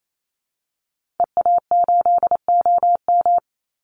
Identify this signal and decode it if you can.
Morse code, heard as a keyed tone, about 28 wpm, 700 Hz. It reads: EA8OM
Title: Call Signs Repeated 3x - 28wpm